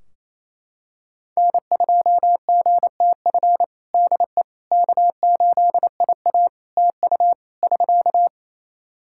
Morse code, heard as a keyed tone, about 28 wpm, 700 Hz. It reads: N2GTF DE K8IA TU <SK>
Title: Practice QSO - Exchange 4 - Ending - 28wpm